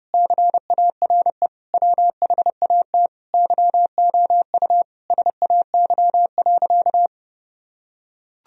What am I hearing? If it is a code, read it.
Morse code, heard as a keyed tone, about 30 wpm, 700 Hz. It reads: CARE WHAT YOU SAY.
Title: Sentences from Top 1000 Words - Review Part 1 - 30wpm